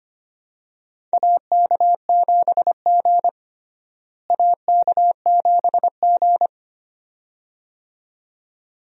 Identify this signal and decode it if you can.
Morse code, heard as a keyed tone, about 25 wpm, 700 Hz. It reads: AK7G AK7G
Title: USA Call Signs - 4 Characters - Repeated 2x - 25wpm